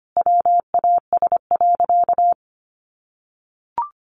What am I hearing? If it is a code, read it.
Morse code, heard as a keyed tone, about 25 wpm, 700 Hz. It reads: WAS.
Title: Sentences from Top 800 Words 25wpm